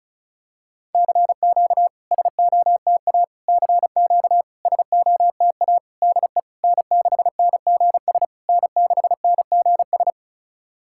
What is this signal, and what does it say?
Morse code, heard as a keyed tone, about 36 wpm, 700 Hz. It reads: CQ SOTA CQ SOTA DE N6NGS N6NGS
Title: SOTA QSO Practice 35wpm